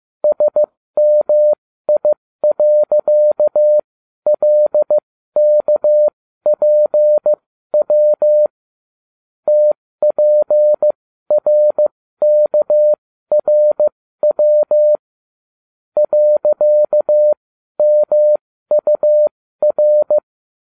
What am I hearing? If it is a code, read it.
Morse code, heard as a keyed tone, about 15 wpm, spaced as for 12 wpm, 600 Hz. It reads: SMI.LKPW TPRKRW .MUR
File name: cw / ubacw12 2